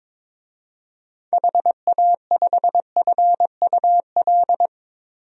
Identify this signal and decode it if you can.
Morse code, heard as a keyed tone, about 22 wpm, 700 Hz. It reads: HA5FUL
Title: DXCC-100 Call Signs - 6 Characters - 22wpm